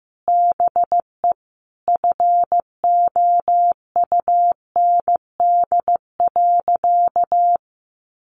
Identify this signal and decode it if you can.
Morse code, heard as a keyed tone, about 14 wpm, 700 Hz. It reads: BE FOUND.
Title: Sentences from Top 1500 Words 15wpm